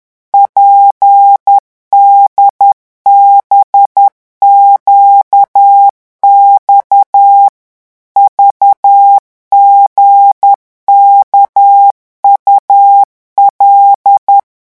Morse code, heard as a keyed tone, about 10 wpm, 800 Hz. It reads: PDBQX VGKUL
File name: morse mp3 / revision11 lettres vitesse 10